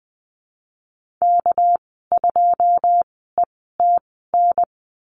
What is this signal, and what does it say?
Morse code, heard as a keyed tone, about 20 wpm, spaced as for 14 wpm, 700 Hz. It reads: K2ETN
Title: 213 Lesson 34g - Number 8 - Call Signs 20wpm